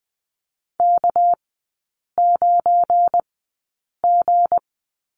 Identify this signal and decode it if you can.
Morse code, heard as a keyed tone, about 20 wpm, spaced as for 8 wpm, 690 Hz. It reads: K9G
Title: 212 Lesson 34e - Number 8 - Three Characters 20wpm